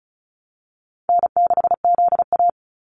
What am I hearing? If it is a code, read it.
Morse code, heard as a keyed tone, about 34 wpm, 700 Hz. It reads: N6ZA
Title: NA SOTA Activator Callsigns 35wpm